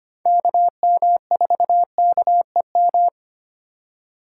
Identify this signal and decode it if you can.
Morse code, heard as a keyed tone, about 25 wpm, 700 Hz. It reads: KM4KEM
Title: USA Call Signs - 6 Characters - 2x Character Spacing - 25wpm